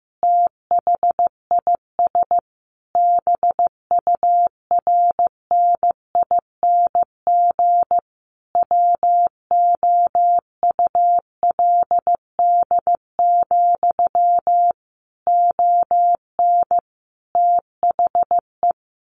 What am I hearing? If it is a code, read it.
Morse code, heard as a keyed tone, about 15 wpm, 700 Hz. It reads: THIS BURNING WOULD, ON THE
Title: A Sound Of Thunder 15wpm